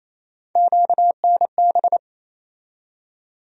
Morse code, heard as a keyed tone, about 28 wpm, 700 Hz. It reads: QNB
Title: USA Call Sign Postfixes - 3 Letters 28wpm